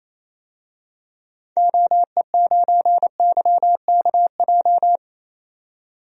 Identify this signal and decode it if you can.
Morse code, heard as a keyed tone, about 28 wpm, 700 Hz. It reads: OE9YKJ
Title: DXCC-100 Call Signs - 6 Characters - 28wpm